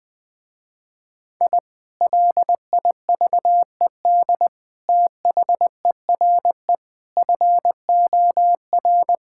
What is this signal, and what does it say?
Morse code, heard as a keyed tone, about 20 wpm, 700 Hz. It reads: I LIVED THERE FOR
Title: Sentences with 3-Word Phrases - Spoken Beginning - 20wpm